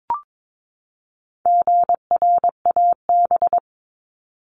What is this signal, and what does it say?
Morse code, heard as a keyed tone, about 22 wpm, 700 Hz. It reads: GRAB
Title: Top 1751-2000 Words 22wpm